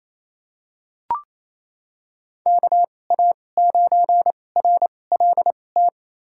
Morse code, heard as a keyed tone, about 28 wpm, spaced as for 20 wpm, 700 Hz. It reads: KA9RLT
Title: USA Call Signs - 6 Characters - 2x Character Spacing - 28wpm